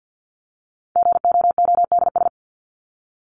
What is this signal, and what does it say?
Morse code, heard as a keyed tone, about 50 wpm, 700 Hz. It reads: GOODS
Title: ES:S Suffix - Top 100 Words with Base Words 50wpm